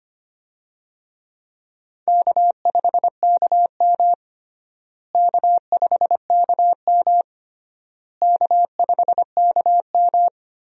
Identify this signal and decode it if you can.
Morse code, heard as a keyed tone, about 25 wpm, 700 Hz. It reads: K5KM K5KM K5KM
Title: USA Call Signs - 4 Characters - Repeated 3x - 25wpm